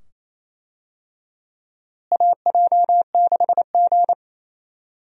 Morse code, heard as a keyed tone, about 28 wpm, 700 Hz. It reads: AJ6G
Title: USA Call Signs - 4 Characters - Repeated 2x - 28wpm